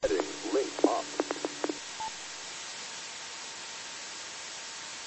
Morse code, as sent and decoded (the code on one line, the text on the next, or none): none